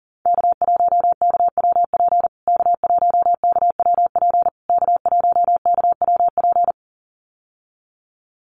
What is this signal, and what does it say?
Morse code, heard as a keyed tone, about 40 wpm, 700 Hz. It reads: K1KWP K1KWP K1KWP
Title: USA Call Signs Repeated 3x 40wpm